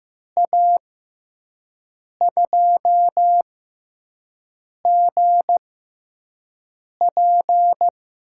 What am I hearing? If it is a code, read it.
Morse code, heard as a keyed tone, about 15 wpm, spaced as for 5 wpm, 700 Hz. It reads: A2GP
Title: USA Call Signs - 5 Characters - 6x Character Spacing - 15wpm